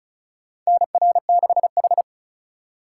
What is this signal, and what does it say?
Morse code, heard as a keyed tone, about 36 wpm, 700 Hz. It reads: NR6H
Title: USA Call Signs - 4x Character Spacing - 35wpm